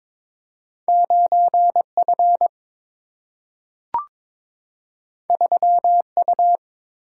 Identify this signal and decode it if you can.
Morse code, heard as a keyed tone, about 22 wpm, 700 Hz. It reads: 9 F 3 U
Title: Single-Letter Single-Number 22wpm